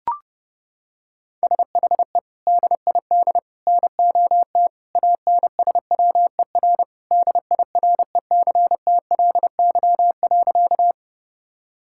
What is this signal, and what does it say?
Morse code, heard as a keyed tone, about 30 wpm, 700 Hz. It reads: SHE DID NOT ANSWER DIRECTLY.
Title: Sentences from Top 1000 Words - Review Part 5 - 30wpm